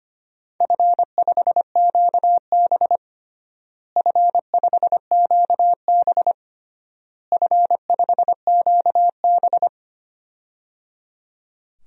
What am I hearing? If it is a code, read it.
Morse code, heard as a keyed tone, about 25 wpm, 700 Hz. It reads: F5QB F5QB F5QB
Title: DXCC-100 Call Signs - 4 Characters - Repeated 3x - 25wpm